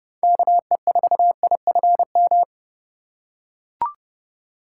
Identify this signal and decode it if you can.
Morse code, heard as a keyed tone, about 28 wpm, 700 Hz. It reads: KE4IFM